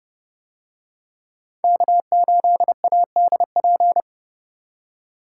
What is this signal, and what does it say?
Morse code, heard as a keyed tone, about 29 wpm, 700 Hz. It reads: K8ADP